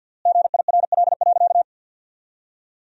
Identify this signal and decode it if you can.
Morse code, heard as a keyed tone, about 50 wpm, 700 Hz. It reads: GIRL.
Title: 3x Word-Spacing - Sentences from Top 400 Words - 50wpm